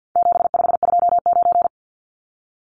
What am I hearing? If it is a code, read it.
Morse code, heard as a keyed tone, about 50 wpm, 700 Hz. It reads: 7529
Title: Numbers - 4-Digit 50wpm